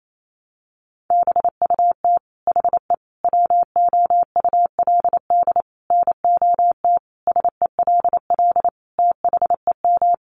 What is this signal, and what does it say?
Morse code, heard as a keyed tone, about 28 wpm, 700 Hz. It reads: BUT HE WOULD NOT SELL THEM
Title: Sentences from Top 1100 Words 28wpm